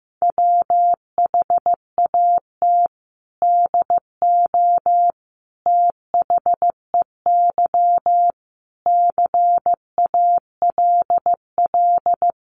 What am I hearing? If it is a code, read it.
Morse code, heard as a keyed tone, about 15 wpm, 700 Hz. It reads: WHAT DO THEY CALL
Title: Sentences from Top 100 Words 15wpm